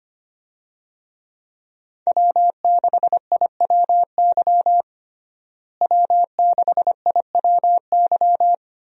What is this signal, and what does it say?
Morse code, heard as a keyed tone, about 25 wpm, 700 Hz. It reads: W6IWY W6IWY
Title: USA Call Signs - 5 Characters - Repeated 2x - 25wpm